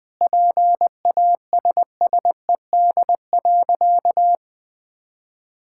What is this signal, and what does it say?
Morse code, heard as a keyed tone, about 21 wpm, 700 Hz. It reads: PASSED.
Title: Sentences with 4-Word Phrases - 20wpm